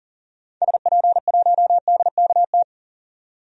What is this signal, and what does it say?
Morse code, heard as a keyed tone, about 40 wpm, 700 Hz. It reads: SP1DKT
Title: DXCC-100 Call Signs - 6 Characters - 40wpm